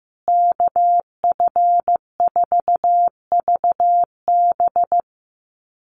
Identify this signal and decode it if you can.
Morse code, heard as a keyed tone, about 15 wpm, 700 Hz. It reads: KF4VB
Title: USA Call Signs Repeated 2x 15wpm